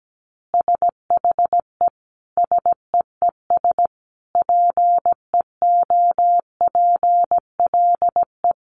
Morse code, heard as a keyed tone, about 17 wpm, 700 Hz. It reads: SHE SEES PEOPLE
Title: Easy Phrases 17wpm